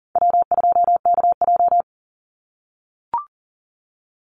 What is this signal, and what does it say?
Morse code, heard as a keyed tone, about 43 wpm, 700 Hz. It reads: W2KJ
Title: Call Signs - 40wpm